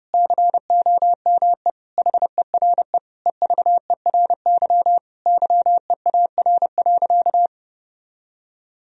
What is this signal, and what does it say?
Morse code, heard as a keyed tone, about 30 wpm, 700 Hz. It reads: COME HERE EVERY YEAR.
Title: Sentences from Top 1000 Words - Review Part 1 - 30wpm